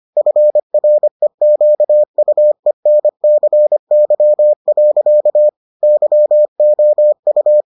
Morse code, heard as a keyed tone, about 25 wpm, 600 Hz. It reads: FREQUENCY. YOU